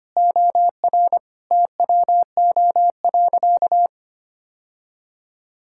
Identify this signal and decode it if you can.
Morse code, heard as a keyed tone, about 25 wpm, 700 Hz. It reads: OR TWO.
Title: Sentences from Top 1600 Words 25wpm